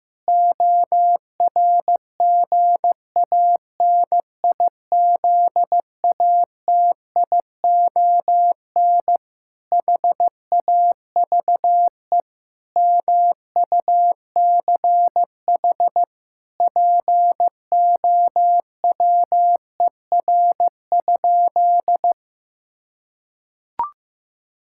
Morse code, heard as a keyed tone, about 15 wpm, 700 Hz. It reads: ORGANIZATION HAVE MUCH POWER?
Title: Sentences from Top 1000 Words - Review Part 5 - 15wpm